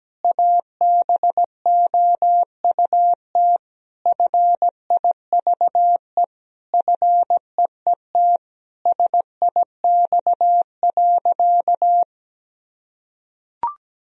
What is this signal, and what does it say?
Morse code, heard as a keyed tone, about 17 wpm, 700 Hz. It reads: ABOUT FIVE FEET SIX.
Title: Sentences from Top 1000 Words - Review Part 2 - 17wpm